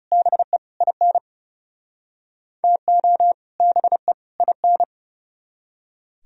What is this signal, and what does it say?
Morse code, heard as a keyed tone, about 40 wpm, 700 Hz. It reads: BE IN TO BE IN